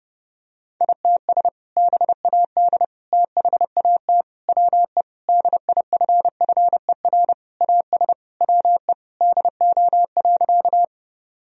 Morse code, heard as a keyed tone, about 32 wpm, 700 Hz. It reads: ITS BAD THAT WE DIFFER AS WE DO.
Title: Sentences from Top 1000 Words - Review Part 1 - 30wpm